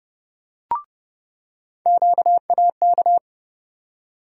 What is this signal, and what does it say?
Morse code, heard as a keyed tone, about 30 wpm, 700 Hz. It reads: QAK
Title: USA Call Sign Postfixes - 3 Letters 30wpm